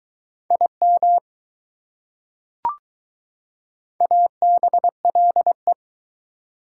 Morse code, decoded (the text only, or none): IM ABLE